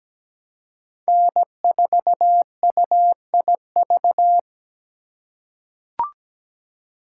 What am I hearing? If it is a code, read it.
Morse code, heard as a keyed tone, about 17 wpm, 700 Hz. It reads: N4UIV
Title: USA Call Signs - 3x Character Spacing - 17wpm